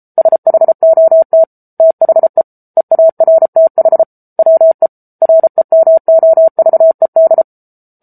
Morse code, heard as a keyed tone, about 34 wpm, 650 Hz. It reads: SHOT THE EARTH WE REMOVED
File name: Book Two / 07-Chapter 7 - 33 WPM